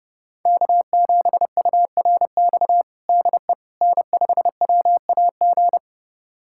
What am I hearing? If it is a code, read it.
Morse code, heard as a keyed tone, about 30 wpm, 700 Hz. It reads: K7URX DE N5WAG
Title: USA Callsign DE Callsign - 5 Characters - 30wpm